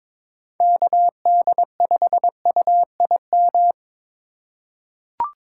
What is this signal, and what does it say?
Morse code, heard as a keyed tone, about 22 wpm, 700 Hz. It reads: KD5UIM
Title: USA Call Signs - 6 Characters - 4x Character Spacing - 22wpm